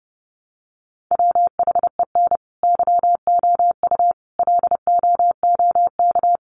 Morse code, heard as a keyed tone, about 30 wpm, 700 Hz. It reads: WHEN YOU LOOK
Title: Sets of 5 Words 30wpm